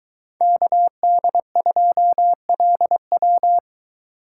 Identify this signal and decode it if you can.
Morse code, heard as a keyed tone, about 21 wpm, 700 Hz. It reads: KD2LW